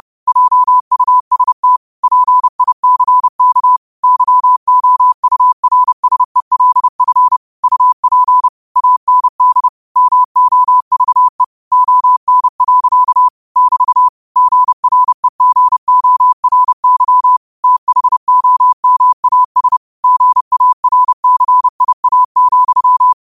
Morse code, heard as a keyed tone, about 30 wpm, 1000 Hz. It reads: JUST PICK YOURSELF UP AND MOVE ON. <BT> GREGORY THOMAS GARCIA,
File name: qotdfiles / 30wpm